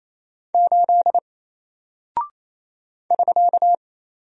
Morse code, heard as a keyed tone, about 27 wpm, 700 Hz. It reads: 8 <SK>
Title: Single Character (Letter:Number:Punctuation:Prosign) 28wpm